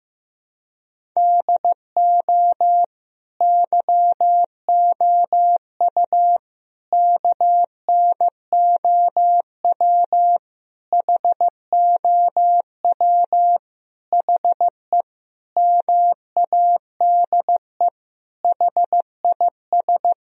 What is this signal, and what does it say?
Morse code, heard as a keyed tone, about 15 wpm, 700 Hz. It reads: DO YOU KNOW HOW HE MADE HIS
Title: Sentences from Top 1000 Words - Review Part 1 - 15wpm